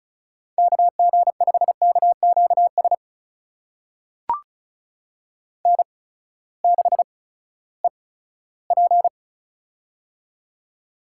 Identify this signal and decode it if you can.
Morse code, heard as a keyed tone, about 36 wpm, spaced as for 18 wpm, 700 Hz. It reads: KG5KQS N6EP
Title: USA Call Signs - 8x Character Spacing - 35wpm